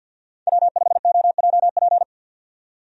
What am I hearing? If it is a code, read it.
Morse code, heard as a keyed tone, about 53 wpm, 700 Hz. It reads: W5OJP
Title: USA Call Signs - 5 Characters - 50wpm